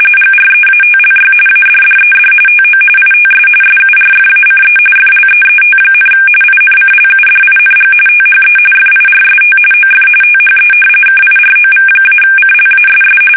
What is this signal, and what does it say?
RTTY, about 45.45 baud. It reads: GREATLY ADDED TO ITS SURFACE FLEET, INCLUDING THE FIRST OF THE KIROV-CLASS
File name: fsk850 / battleshipnj